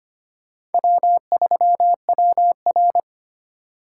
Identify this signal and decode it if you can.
Morse code, heard as a keyed tone, about 30 wpm, 700 Hz. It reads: W3WR
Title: Speed Racing - USA 1x2 Call Signs 25wpm